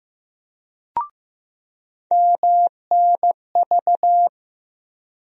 Morse code, heard as a keyed tone, about 15 wpm, 700 Hz. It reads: MNV
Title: USA Call Sign Postfixes - 3 Letters 15wpm